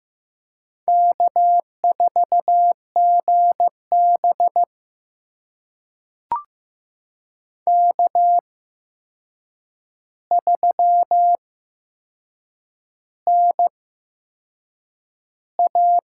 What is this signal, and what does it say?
Morse code, heard as a keyed tone, about 15 wpm, spaced as for 6 wpm, 700 Hz. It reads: K4GB K3NA
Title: USA Call Signs - 8x Character Spacing - 15wpm